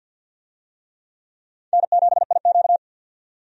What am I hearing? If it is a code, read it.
Morse code, heard as a keyed tone, about 52 wpm, 700 Hz. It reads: N7IQ
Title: USA Call Signs - 4 Characters - 50wpm